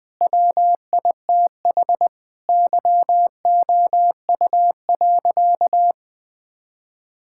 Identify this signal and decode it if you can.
Morse code, heard as a keyed tone, about 21 wpm, 700 Hz. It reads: WITH YOU.
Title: Sentences from Top 1000 Words - Review Part 6 - 20wpm